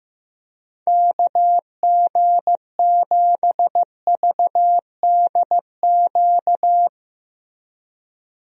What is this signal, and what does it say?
Morse code, heard as a keyed tone, about 15 wpm, 700 Hz. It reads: KG7VDQ
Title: USA Call Signs - 6 Characters - Repeated 2x - 15wpm